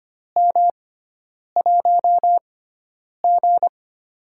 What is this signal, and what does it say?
Morse code, heard as a keyed tone, about 25 wpm, spaced as for 9 wpm, 700 Hz. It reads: M1G
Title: USA Call Signs - 4 Characters - 6x Character Spacing - 25wpm